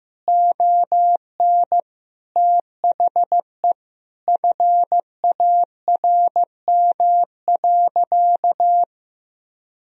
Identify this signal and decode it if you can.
Morse code, heard as a keyed tone, about 15 wpm, 700 Hz. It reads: ON THE FARM.
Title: Sentences from Top 1000 Words - Review Part 4 - 15wpm